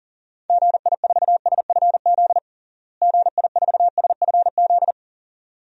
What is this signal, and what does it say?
Morse code, heard as a keyed tone, about 40 wpm, 700 Hz. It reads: GI4SFZ GI4SFZ
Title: DXCC-100 Call Signs - 6 Characters - Repeated 2x - 40wpm